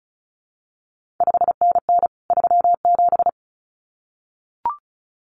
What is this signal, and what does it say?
Morse code, heard as a keyed tone, about 36 wpm, 700 Hz. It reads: 5NN 37
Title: CQ WW DX Contest Exchange Encore - 35wpm